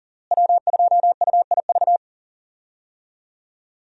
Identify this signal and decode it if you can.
Morse code, heard as a keyed tone, about 39 wpm, 700 Hz. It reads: W2UIV
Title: USA Call Signs - 5 Characters - Repeated 3x - 40wpm